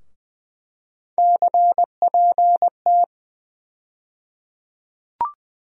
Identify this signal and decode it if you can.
Morse code, heard as a keyed tone, about 20 wpm, 700 Hz. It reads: CPT
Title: USA Call Sign Postfixes - 3 Letters 20wpm